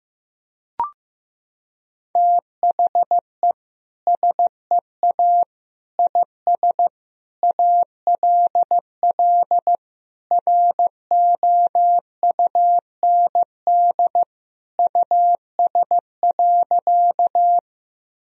Morse code, heard as a keyed tone, about 15 wpm, 700 Hz. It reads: THE SEA IS ALL ROUND US.
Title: Sentences from Top 1000 Words - Review Part 1 - 15wpm